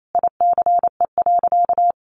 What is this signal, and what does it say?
Morse code, heard as a keyed tone, about 28 wpm, 700 Hz. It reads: ICE.
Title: Sentences from Top 1000 Words 28wpm